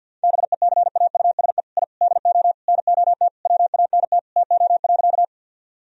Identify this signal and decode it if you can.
Morse code, heard as a keyed tone, about 52 wpm, 700 Hz. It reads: BECAUSE I DO NOT WANT TO.
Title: Sentences from Top 1000 Words - Review Part 3 - 50wpm